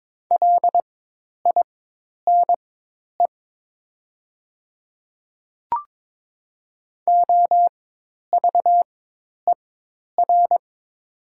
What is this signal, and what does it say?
Morse code, heard as a keyed tone, about 22 wpm, spaced as for 10 wpm, 700 Hz. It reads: LINE OVER
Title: 4x Character Spacing - Top 200 Words - 22wpm